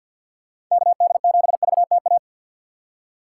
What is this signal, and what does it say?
Morse code, heard as a keyed tone, about 51 wpm, 700 Hz. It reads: KD7VTA
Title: USA Call Signs - 3x Character Spacing - 50wpm